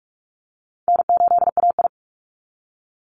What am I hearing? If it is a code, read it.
Morse code, heard as a keyed tone, about 44 wpm, 700 Hz. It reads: N8AI
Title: Easy Call Signs Repeated 3x - 45wpm